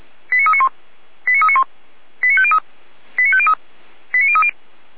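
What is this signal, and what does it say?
Five-tone selective call (ZVEI), heard as quick runs of five tones, tones of about 70 ms. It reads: 89391 89391 89503 89503 89030
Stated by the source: ZVEI five-tone selective call 89187 received on SDR.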